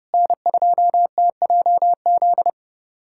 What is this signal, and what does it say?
Morse code, heard as a keyed tone, about 30 wpm, 700 Hz. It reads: N2TJZ
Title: USA Callsign DE Callsign 30wpm